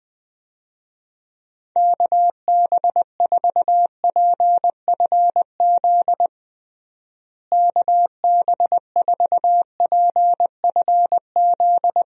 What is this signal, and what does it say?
Morse code, heard as a keyed tone, about 20 wpm, 700 Hz. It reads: KB4PFZ KB4PFZ
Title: USA Call Signs - 6 Characters - Repeated 2x - 20wpm